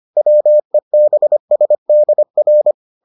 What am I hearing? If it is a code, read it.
Morse code, heard as a keyed tone, about 25 wpm, 600 Hz. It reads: WEBSDR